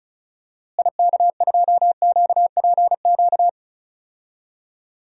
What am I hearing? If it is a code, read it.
Morse code, heard as a keyed tone, about 36 wpm, 700 Hz. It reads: IK2QPQ
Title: DXCC-100 Call Signs - 6 Characters - Repeated 2x - 35wpm